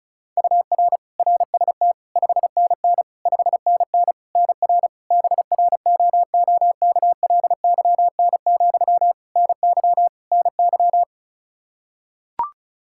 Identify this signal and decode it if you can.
Morse code, heard as a keyed tone, about 36 wpm, 700 Hz. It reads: UR RST 5NN 5NN NR BROOKLYN, NY NY
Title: K1USN SST QSO Practice 35wpm